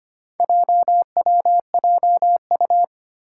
Morse code, heard as a keyed tone, about 25 wpm, 700 Hz. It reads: JWJU
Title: Four Challenging Letter-Number Combinations 25wpm